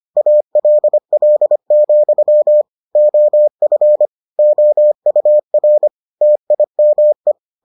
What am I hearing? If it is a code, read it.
Morse code, heard as a keyed tone, about 25 wpm, 600 Hz. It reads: ALL, OF OUR TIME